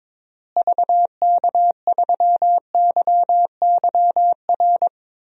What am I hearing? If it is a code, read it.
Morse code, heard as a keyed tone, about 21 wpm, 700 Hz. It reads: VK3YYR
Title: DXCC-100 Call Signs - 6 Characters - Repeated 2x - 22wpm